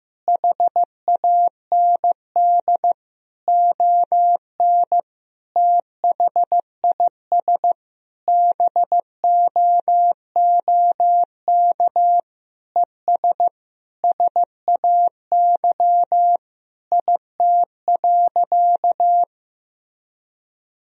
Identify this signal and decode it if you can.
Morse code, heard as a keyed tone, about 15 wpm, 700 Hz. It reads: HAND ON THIS BOOK ES SAY IT.
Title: Sentences from Top 400 Words and Abbreviations 15wpm